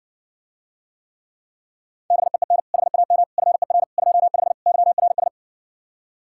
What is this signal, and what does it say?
Morse code, heard as a keyed tone, about 60 wpm, 700 Hz. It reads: BEEN HAM FER 25 YRS